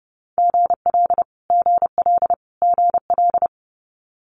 Morse code, heard as a keyed tone, about 30 wpm, 700 Hz. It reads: GL GL GL
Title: Multi-letter Phonemes Repeated 3x 30wpm